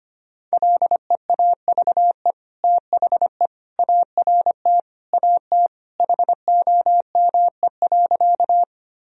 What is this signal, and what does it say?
Morse code, heard as a keyed tone, about 25 wpm, 700 Hz. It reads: LEAVE THE ART AT HOME.
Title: Sentences from Top 1000 Words - Review Part 3 - 25wpm